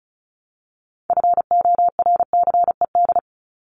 Morse code, heard as a keyed tone, about 34 wpm, 700 Hz. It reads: FORCED
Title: ED Suffix - Top 100 Words with Base Words 35wpm